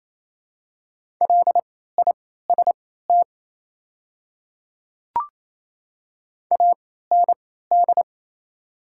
Morse code, heard as a keyed tone, about 28 wpm, spaced as for 16 wpm, 700 Hz. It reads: LIST AND
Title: 3x Character Spacing - Top 500 Words - 28wpm